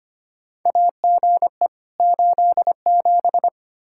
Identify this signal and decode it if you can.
Morse code, heard as a keyed tone, about 25 wpm, 700 Hz. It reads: AGE 87
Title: K1USN SST QSO Practice 25wpm